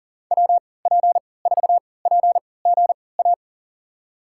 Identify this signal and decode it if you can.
Morse code, heard as a keyed tone, about 40 wpm, spaced as for 23 wpm, 700 Hz. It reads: WP4PGA
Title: USA Call Signs - 6 Characters - 3x Character Spacing - 40wpm